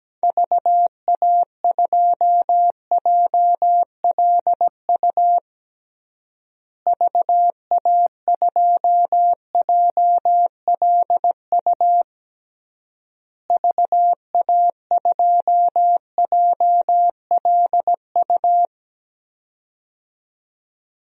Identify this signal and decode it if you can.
Morse code, heard as a keyed tone, about 17 wpm, 700 Hz. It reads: VA2JLU VA2JLU VA2JLU
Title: DXCC-100 Call Signs - 6 Characters - Repeated 3x - 17wpm